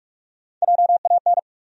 Morse code, heard as a keyed tone, about 45 wpm, 700 Hz. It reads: JAN
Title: USA Common Names - Max 4 Letters - 45wpm